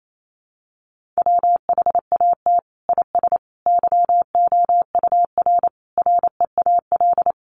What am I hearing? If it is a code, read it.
Morse code, heard as a keyed tone, about 28 wpm, 700 Hz. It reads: WHAT IS YOUR REAL
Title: Sentences from Top 300 Words 28wpm